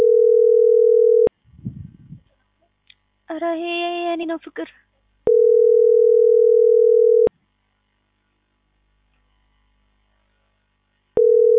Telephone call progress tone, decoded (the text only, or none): ringback tone